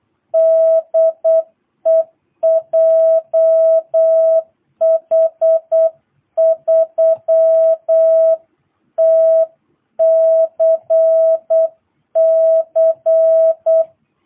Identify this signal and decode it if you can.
Morse code, heard as a keyed tone, about 8 wpm, 660 Hz. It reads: DE JH3TCC
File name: jh8cbh / 2022cw